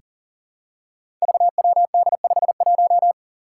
Morse code, heard as a keyed tone, about 40 wpm, 700 Hz. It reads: VWD51
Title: Five Challenging Letter-Number Combinations 40wpm